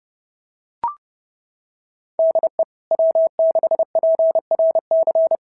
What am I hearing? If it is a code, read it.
Morse code, heard as a keyed tone, about 30 wpm, 690 Hz. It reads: DE W6PRC
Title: SOTA QSO Practice 30wpm